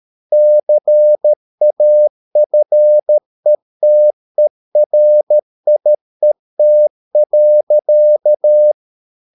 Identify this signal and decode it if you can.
Morse code, heard as a keyed tone, about 13 wpm, 600 Hz. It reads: CAFETERIET.
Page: Ferie 13 wpm